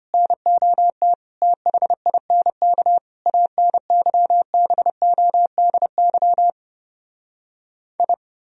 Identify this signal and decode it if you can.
Morse code, heard as a keyed tone, about 33 wpm, 700 Hz. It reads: NOT THINK ANYBODY I